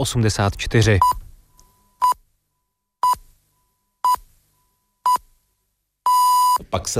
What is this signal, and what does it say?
Radio time signal (time pips), with speech. It is heard from 1.0 to 6.6 s.